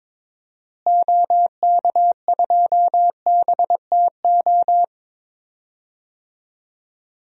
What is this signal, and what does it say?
Morse code, heard as a keyed tone, about 21 wpm, 700 Hz. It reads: OK2BTO
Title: DXCC-100 Call Signs - 6 Characters - Repeated 2x - 22wpm